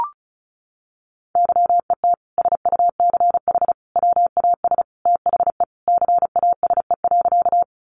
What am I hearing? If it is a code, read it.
Morse code, heard as a keyed tone, about 34 wpm, 700 Hz. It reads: YET SUCH WAS THE CASE.
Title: Sentences from Top 700 Words 35wpm